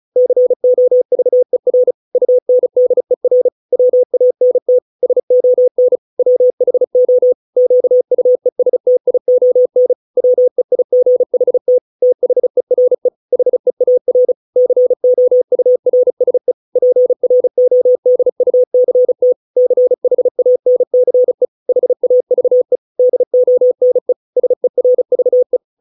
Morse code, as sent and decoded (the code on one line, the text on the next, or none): -.-. --- ...- . .-. / ..- -. -.. .-. / .-- .- -. - / ... --- -. / .-- .... --- / --.- ..- . ... - .. --- -. / .-- . .. --. .... - / - .... .-. / .... .- .-. / -.-. --- ..- .-. ... / .--. .-. --- -.. ..- -.-. - / -.-. .... .- -. -.-. / .... .- ...- . / -.. --- -. / ... .-. ...- .
COVER UNDER WANT SON WHO QUESTION WEIGHT THERE HEAR COURSE PRODUCT CHANCE HAVE DONE SERVE